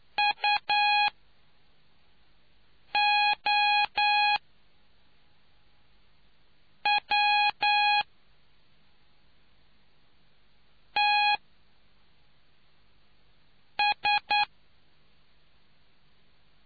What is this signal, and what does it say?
Morse code, heard as a keyed tone, about 9 wpm, spaced as for 3 wpm, 830 Hz. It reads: UOWTS